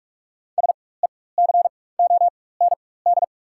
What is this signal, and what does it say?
Morse code, heard as a keyed tone, about 47 wpm, spaced as for 21 wpm, 690 Hz. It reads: SECOND
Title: 4x Character Spacing - Top 500 Words - 45wpm